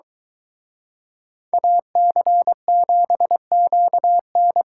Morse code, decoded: AC7QN

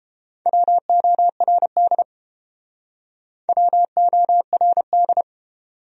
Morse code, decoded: WORD WORD

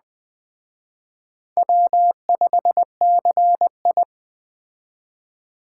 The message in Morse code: W5CI

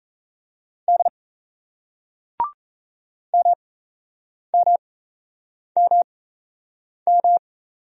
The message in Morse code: D MMMM

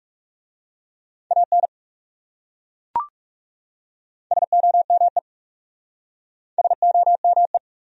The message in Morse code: AN SOME SOME

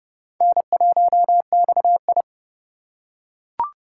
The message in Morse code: N1XI

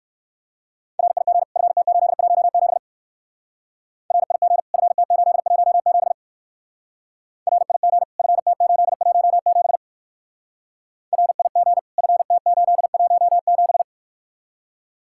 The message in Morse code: RIG FT817 RIG FT817 RIG FT817 RIG FT817

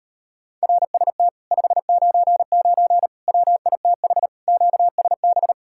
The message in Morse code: RST 599 WITH QSB

